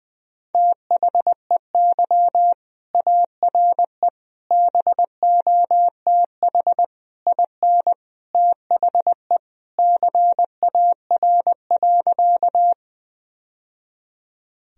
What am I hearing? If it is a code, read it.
Morse code, heard as a keyed tone, about 20 wpm, 700 Hz. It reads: THEY ARE BOTH IN THE CAR.
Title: Sentences from Top 1000 Words - Review Part 1 - 20wpm